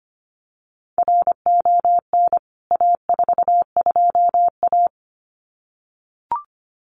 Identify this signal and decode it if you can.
Morse code, heard as a keyed tone, about 25 wpm, 700 Hz. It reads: RON A42A
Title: CWT Exchange 25wpm